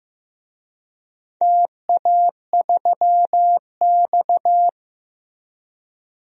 Morse code, heard as a keyed tone, about 16 wpm, 700 Hz. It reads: TA3X
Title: DXCC-100 Call Signs - 4 Characters - Repeated 3x - 15wpm